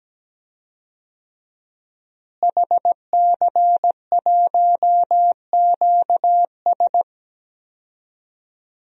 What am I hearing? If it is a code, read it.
Morse code, heard as a keyed tone, about 17 wpm, 700 Hz. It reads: HC1QS
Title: DXCC-100 Call Signs - 5 Characters - Repeated 3x - 17wpm